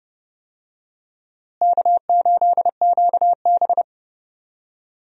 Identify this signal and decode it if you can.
Morse code, heard as a keyed tone, about 30 wpm, 700 Hz. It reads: K8QB
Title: USA Call Signs - 4 Characters - Repeated 3x - 30wpm